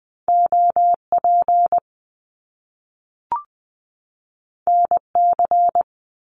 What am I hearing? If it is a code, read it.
Morse code, heard as a keyed tone, about 20 wpm, 700 Hz. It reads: OP NC
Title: Top Two Letter Combinations in Words 20wpm